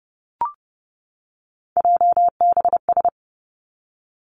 Morse code, heard as a keyed tone, about 30 wpm, 700 Hz. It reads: JBS
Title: Three Challenging Letter-Number Combinations 30wpm